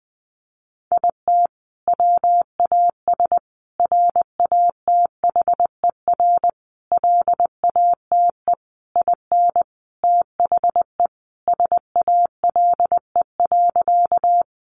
Morse code, heard as a keyed tone, about 20 wpm, 700 Hz. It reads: IT WAS RATHER LATE IN THE SALE.
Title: Sentences from Top 1400 Words 20wpm